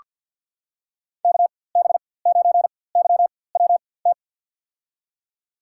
Morse code, heard as a keyed tone, about 47 wpm, spaced as for 23 wpm, 700 Hz. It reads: KB9YWT
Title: USA Call Signs - 4x Character Spacing - 50wpm